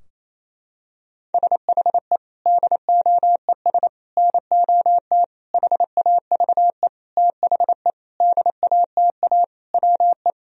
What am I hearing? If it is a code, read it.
Morse code, heard as a keyed tone, about 28 wpm, 700 Hz. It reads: SHE DOES NOT HAVE THE DATA WE
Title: Sentences from Top 1000 Words - Review Part 3 - 28wpm